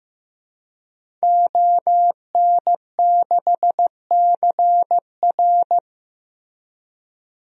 Morse code, heard as a keyed tone, about 15 wpm, 700 Hz. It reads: ON6CR
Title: DXCC-100 Call Signs - 5 Characters - Repeated 2x - 15wpm